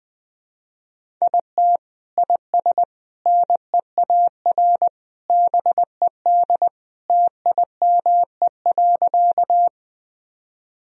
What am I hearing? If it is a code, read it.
Morse code, heard as a keyed tone, about 21 wpm, 700 Hz. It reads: IT IS NEAR BED TIME.
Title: Sentences from Top 1000 Words - Review Part 1 - 20wpm